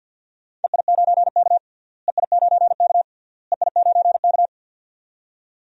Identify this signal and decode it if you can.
Morse code, heard as a keyed tone, about 46 wpm, 700 Hz. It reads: EI9K EI9K EI9K
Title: DXCC-100 Call Signs - 4 Characters - Repeated 3x - 50wpm